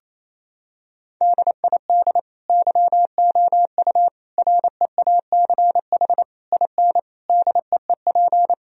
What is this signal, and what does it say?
Morse code, heard as a keyed tone, about 28 wpm, 700 Hz. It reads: DID YOU REACH IN DEEP
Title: Sentences from Top 1000 Words - Review Part 3 - 28wpm